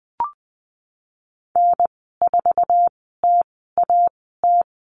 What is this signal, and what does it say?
Morse code, heard as a keyed tone, about 20 wpm, spaced as for 15 wpm, 700 Hz. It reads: N4TAT
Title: 048 Lesson 9g - Number 4 - Call Signs 20wpm